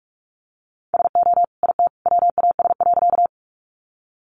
Morse code, heard as a keyed tone, about 44 wpm, 700 Hz. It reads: SO IT WAS.
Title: Sentences from Top 100 Words 45wpm